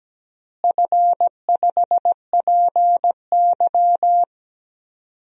Morse code, heard as a keyed tone, about 17 wpm, 700 Hz. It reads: F5PY
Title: DXCC-100 Call Signs - 4 Characters - Repeated 2x - 17wpm